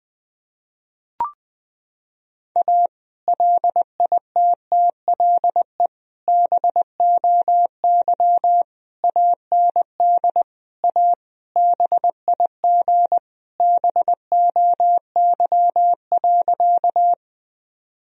Morse code, heard as a keyed tone, about 20 wpm, 700 Hz. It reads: A LITTLE BOY AND A BIG BOY.
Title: Sentences from Top 1000 Words - Review Part 1 - 20wpm